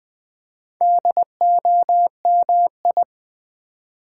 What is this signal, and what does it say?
Morse code, heard as a keyed tone, about 20 wpm, 700 Hz. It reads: DOMI
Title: Build A Word - 4 Letters At A Time - 8-Letter Words - 20wpm